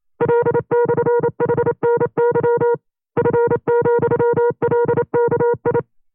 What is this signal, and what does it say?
Morse code, heard as a keyed tone, about 28 wpm, 480 Hz. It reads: L/HNY F,LKI
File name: FreeFiles / CW-practice / 25WPM-side1